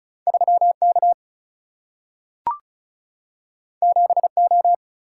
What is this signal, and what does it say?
Morse code, heard as a keyed tone, about 36 wpm, 700 Hz. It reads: 3 K 7 O
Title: Single-Letter Single-Number 35wpm